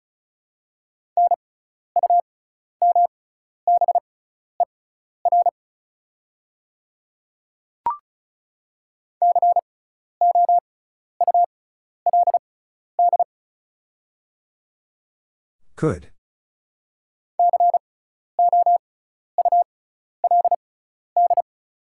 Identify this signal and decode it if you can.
Morse code, heard as a keyed tone, about 36 wpm, spaced as for 12 wpm, 690 Hz. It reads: NUMBER COULD COULD
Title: 6x Character Spacing - Top 100 Words - 35wpm